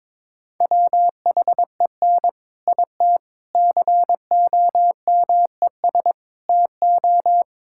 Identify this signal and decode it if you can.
Morse code, heard as a keyed tone, about 22 wpm, 700 Hz. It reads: WHEN IT COMES TO
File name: practice / 22wpm / Sentences with 3-Word Phrases - 22wpm